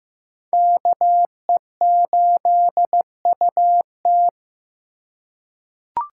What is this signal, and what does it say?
Morse code, heard as a keyed tone, about 15 wpm, 700 Hz. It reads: KE8UT
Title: USA Call Signs - 6x Character Spacing - 15wpm